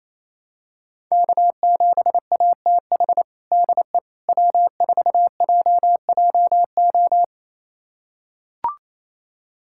Morse code, heard as a keyed tone, about 27 wpm, 690 Hz. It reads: K7ATH DE W4JJO
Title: USA Callsign DE Callsign - 5 Characters - 28wpm